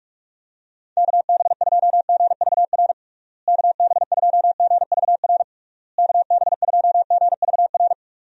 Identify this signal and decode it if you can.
Morse code, heard as a keyed tone, about 42 wpm, 700 Hz. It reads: KB2GVR KB2GVR KB2GVR
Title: USA Call Signs - 6 Characters - Repeated 3x - 45wpm